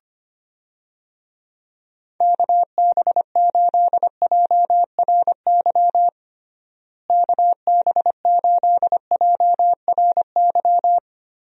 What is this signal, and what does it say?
Morse code, heard as a keyed tone, about 25 wpm, 700 Hz. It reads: KB8JRY KB8JRY
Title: USA Call Signs - 6 Characters - Repeated 2x - 25wpm